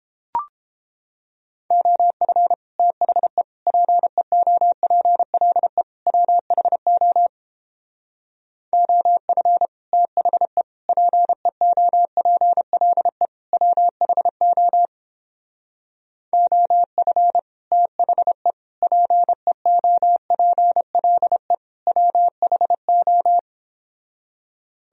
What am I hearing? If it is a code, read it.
Morse code, heard as a keyed tone, about 29 wpm, 700 Hz. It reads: OF THE PEOPLE WHO OF THE PEOPLE WHO OF THE PEOPLE WHO